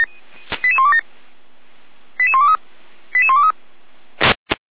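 Five-tone selective call (ZVEI), heard as quick runs of five tones, tones of about 80 ms. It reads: 88127 88123 88123
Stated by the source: ZVEI five-tone call received by SDR, from feuerwehr-orlamuende.org fire brigade audio archive.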